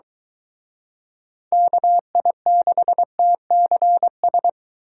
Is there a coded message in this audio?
Yes: Morse code from 1.5 to 4.5 s, heard as a keyed tone, about 23 wpm, 700 Hz. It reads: KI6TCS